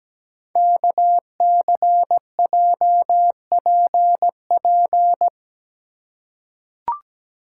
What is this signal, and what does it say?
Morse code, heard as a keyed tone, about 17 wpm, 700 Hz. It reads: KCJPP
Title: Code Groups - 5 Letters - 17wpm